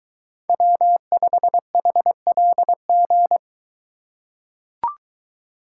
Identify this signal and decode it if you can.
Morse code, heard as a keyed tone, about 22 wpm, 700 Hz. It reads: W5HLG